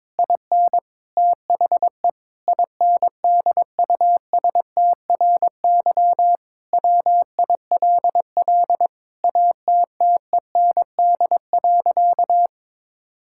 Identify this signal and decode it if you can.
Morse code, heard as a keyed tone, about 22 wpm, 700 Hz. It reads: IN THE INDUSTRY WILL ATTEND.
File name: Sentences with 7-Word Phrases - 22wpm